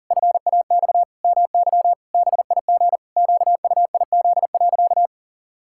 Morse code, heard as a keyed tone, about 40 wpm, 700 Hz. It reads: FAX MY BIG QUIZ.
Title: Pangram Challenge v2 - 40wpm